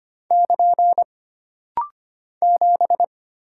Morse code, heard as a keyed tone, about 25 wpm, 700 Hz. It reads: <KN> 7
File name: Single Character (Letter:Number:Punctuation:Prosign) - Rapid-Fire - 25wpm